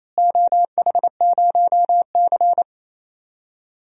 Morse code, heard as a keyed tone, about 28 wpm, 700 Hz. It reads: OH0C
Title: DXCC-100 Call Signs - 4 Characters - Repeated 2x - 28wpm